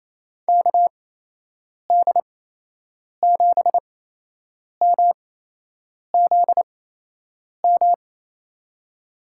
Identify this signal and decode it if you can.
Morse code, heard as a keyed tone, about 27 wpm, spaced as for 8 wpm, 700 Hz. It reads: KD7MZM
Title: USA Call Signs - 6 Characters - 8x Character Spacing - 28wpm